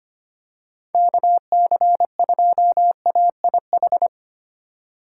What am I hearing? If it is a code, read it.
Morse code, heard as a keyed tone, about 25 wpm, 700 Hz. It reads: KC2AIH
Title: USA Call Signs - 4x Character Spacing - 25wpm